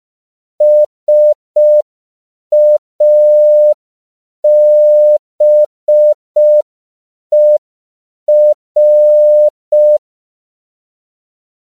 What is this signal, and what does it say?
Morse code, heard as a keyed tone, about 5 wpm, 600 Hz. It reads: SABER
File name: 5PPM